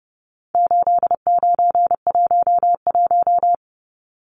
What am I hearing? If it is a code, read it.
Morse code, heard as a keyed tone, about 30 wpm, 700 Hz. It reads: 8911
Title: Numbers - 4-Digit 30wpm